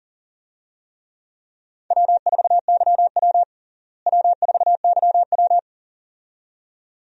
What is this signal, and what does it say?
Morse code, heard as a keyed tone, about 40 wpm, 700 Hz. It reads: W4YW W4YW
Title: USA Call Signs - 4 Characters - Repeated 2x - 40wpm